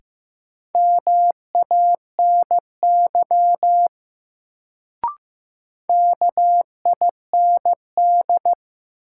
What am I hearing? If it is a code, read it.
Morse code, heard as a keyed tone, about 15 wpm, 700 Hz. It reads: MANY KIND
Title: Sets of 2 Words - New Word Drill - 15wpm